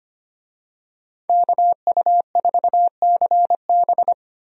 Morse code, heard as a keyed tone, about 25 wpm, 690 Hz. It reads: KU4CB
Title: USA Call Signs - 5 Characters - 25wpm